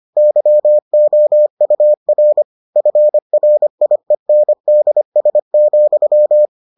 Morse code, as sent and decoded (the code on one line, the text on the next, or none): -.-- --- ..- .-. / ..-. .-. .. -. -.. ... --..--
YOUR FRIENDS,